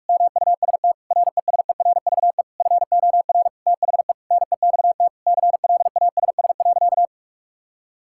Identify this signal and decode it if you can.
Morse code, heard as a keyed tone, about 46 wpm, 700 Hz. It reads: MUST RESERVE FOR THE NEXT CLASS.
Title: Sentences from Top 1000 Words - Review Part 6 - 45wpm